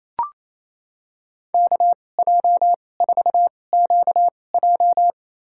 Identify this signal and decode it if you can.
Morse code, heard as a keyed tone, about 28 wpm, spaced as for 20 wpm, 700 Hz. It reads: KJ4QJ
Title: USA Call Signs - 5 Characters - 2x Character Spacing - 28wpm